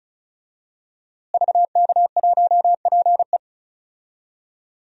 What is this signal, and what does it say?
Morse code, heard as a keyed tone, about 36 wpm, 700 Hz. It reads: VK1PE
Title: DXCC-100 Call Signs - 5 Characters - Repeated 3x - 35wpm